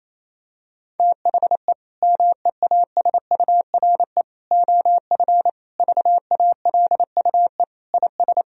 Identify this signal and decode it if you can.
Morse code, heard as a keyed tone, about 27 wpm, 700 Hz. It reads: THE MEASURE OF VALUE IS
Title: Sentences from Top 1000 Words - Review Part 3 - 28wpm